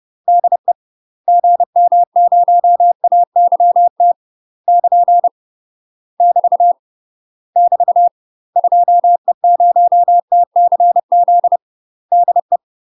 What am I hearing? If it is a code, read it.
Morse code, heard as a keyed tone, about 30 wpm, 700 Hz. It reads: DE GM0AYT <KN> <BT> <BT> 2E0TCZ DE
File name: UK MASTER QSOs TRAINING / 30WPM / UK MASTER QSOs 0028